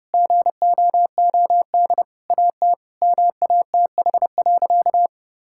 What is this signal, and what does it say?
Morse code, heard as a keyed tone, about 30 wpm, 700 Hz. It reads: GOOD AT MATH.
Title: Sentences from Top 1000 Words - Review Part 6 - 30wpm